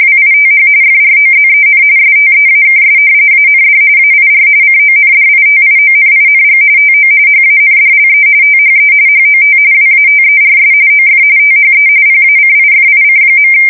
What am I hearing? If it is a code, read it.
RTTY, about 45.45 baud. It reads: WITH FROZEN STEAKS AND AIR CONDITIONING TO HANDLE THE HOT WATERS OF THE PACIFIC.
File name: fsk170 / usnavy4